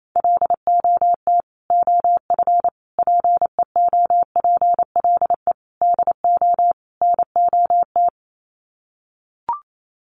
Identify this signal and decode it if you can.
Morse code, heard as a keyed tone, about 28 wpm, 700 Hz. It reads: LOT OF PEOPLE DO NOT
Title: Sets of 5 Words Encore 28wpm